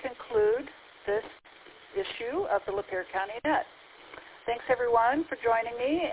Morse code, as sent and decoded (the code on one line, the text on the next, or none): none